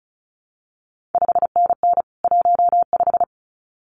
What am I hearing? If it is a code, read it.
Morse code, heard as a keyed tone, about 36 wpm, 700 Hz. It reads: 5NN 15
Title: CQ WW DX Contest Exchange - 35wpm